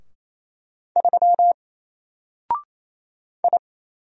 Morse code, heard as a keyed tone, about 27 wpm, 700 Hz. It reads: 3 I
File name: Single-Letter-Number-28wpm